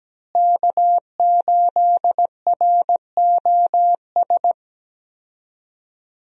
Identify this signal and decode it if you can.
Morse code, heard as a keyed tone, about 17 wpm, 700 Hz. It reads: K8ROS
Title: USA Call Signs - 5 Characters - Repeated 3x - 17wpm